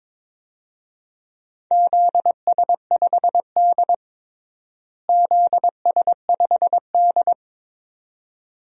Morse code, heard as a keyed tone, about 22 wpm, 700 Hz. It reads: ZS5D ZS5D
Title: DXCC-100 Call Signs - 4 Characters - Repeated 2x - 22wpm